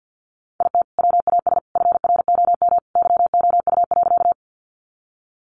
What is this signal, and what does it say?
Morse code, heard as a keyed tone, about 50 wpm, 700 Hz. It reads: IT WAS FROM YOU.
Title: Sentences from Top 100 Words 50wpm